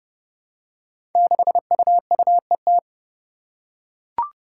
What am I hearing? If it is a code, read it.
Morse code, heard as a keyed tone, about 30 wpm, 700 Hz. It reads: 6UUET
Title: Five Challenging Letter-Number Combinations 30wpm